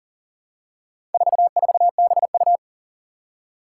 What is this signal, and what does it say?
Morse code, heard as a keyed tone, about 39 wpm, 700 Hz. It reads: 44BU
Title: Four Challenging Letter-Number Combinations 40wpm